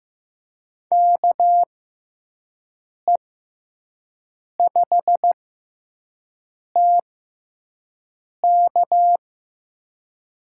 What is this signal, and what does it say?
Morse code, heard as a keyed tone, about 16 wpm, spaced as for 5 wpm, 700 Hz. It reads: KE5TK
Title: USA Call Signs - 6 Characters - 6x Character Spacing - 15wpm